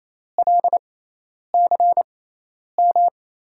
Morse code, heard as a keyed tone, about 28 wpm, spaced as for 10 wpm, 700 Hz. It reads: LCM
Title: USA Call Signs - 6 Characters - 6x Character Spacing - 28wpm